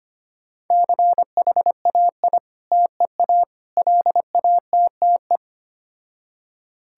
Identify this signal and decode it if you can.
Morse code, heard as a keyed tone, about 25 wpm, 700 Hz. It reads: CHAI TEA LATTE
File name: practice / 25wpm / Difficult Words with Hints - 25wpm